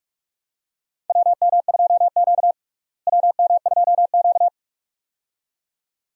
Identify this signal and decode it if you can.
Morse code, heard as a keyed tone, about 41 wpm, 700 Hz. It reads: WM2Q WM2Q
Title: USA Call Signs - 4 Characters - Repeated 2x - 45wpm